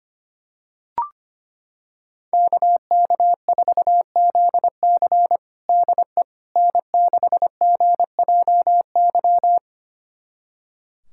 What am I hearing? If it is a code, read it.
Morse code, heard as a keyed tone, about 25 wpm, 700 Hz. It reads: KK4ZC DE N6GJY
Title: USA Callsign DE Callsign - 5 Characters - 25wpm